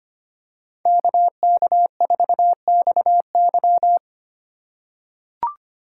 Morse code, heard as a keyed tone, about 25 wpm, 700 Hz. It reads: KK4XY
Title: USA Call Signs - 3x Character Spacing - 25wpm